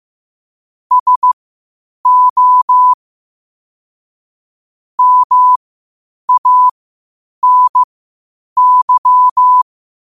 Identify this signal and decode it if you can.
Morse code, heard as a keyed tone, about 15 wpm, spaced as for 8 wpm, 1000 Hz. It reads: SO MANY
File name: qotdfiles / 8wpm